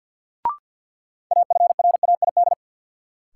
Morse code, heard as a keyed tone, about 53 wpm, 700 Hz. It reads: AFRAID
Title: Sentences from Top 1500 Words - New Word Drill Alt - 50wpm